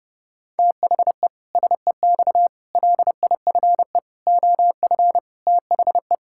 Morse code, heard as a keyed tone, about 32 wpm, 700 Hz. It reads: THE SEX LIFE OF THE
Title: Pangram Challenge v2 - 30wpm